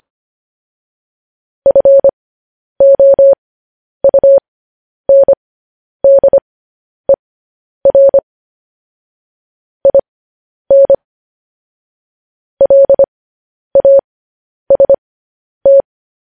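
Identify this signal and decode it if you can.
Morse code, heard as a keyed tone, about 25 wpm, spaced as for 10 wpm, 550 Hz. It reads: FOUNDER IN LAST
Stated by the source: Morse code news headlines practice transmission.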